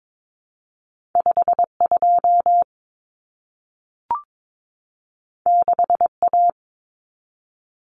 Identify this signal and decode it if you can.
Morse code, heard as a keyed tone, about 22 wpm, 700 Hz. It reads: 52 6A
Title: Common Cut Numbers - 2-Digit 22wpm